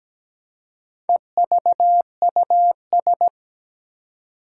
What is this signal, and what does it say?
Morse code, heard as a keyed tone, about 17 wpm, 700 Hz. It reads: EVUS
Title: Four Challenging Letter-Number Combinations 17wpm